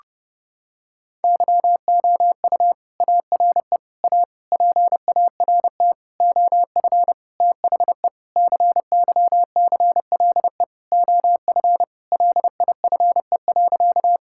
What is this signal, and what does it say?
Morse code, heard as a keyed tone, about 30 wpm, 700 Hz. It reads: YOU ARE A PART OF THE CYCLE OF LIFE.
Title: Sentences from Top 1000 Words - Review Part 6 - 30wpm